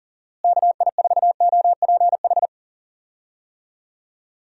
Morse code, heard as a keyed tone, about 40 wpm, 700 Hz. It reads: KI4OPH
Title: USA Call Signs - 6 Characters - Repeated 2x - 40wpm